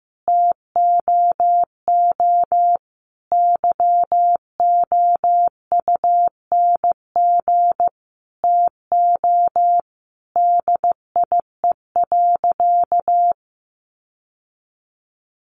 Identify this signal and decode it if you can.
Morse code, heard as a keyed tone, about 15 wpm, 700 Hz. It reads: TOO YOUNG TO DIE.
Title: Sentences from Top 1300 Words 15wpm